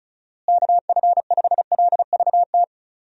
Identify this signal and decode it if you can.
Morse code, heard as a keyed tone, about 36 wpm, 700 Hz. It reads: KF5LVT
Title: USA Call Signs - 6 Characters - Repeated 3x - 35wpm